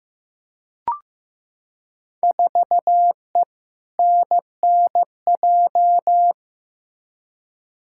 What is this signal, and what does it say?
Morse code, heard as a keyed tone, about 15 wpm, 700 Hz. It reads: 4E NNJ
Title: Field Day Exchanges - 15wpm